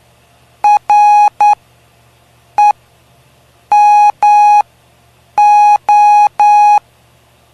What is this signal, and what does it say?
Morse code, heard as a keyed tone, about 9 wpm, spaced as for 6 wpm, 830 Hz. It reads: REMO
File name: cw / py2lcd / 05-PY2LCD-Li��o 5